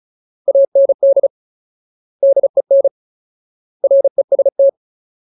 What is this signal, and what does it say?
Morse code, heard as a keyed tone, about 35 wpm, 550 Hz. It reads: AND DEN REST
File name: Ultimate Word List / 35WPM Ultimate